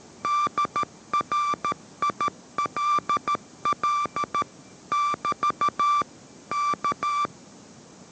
Morse code, heard as a keyed tone, about 13 wpm, 1170 Hz. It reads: DRILL <BT> K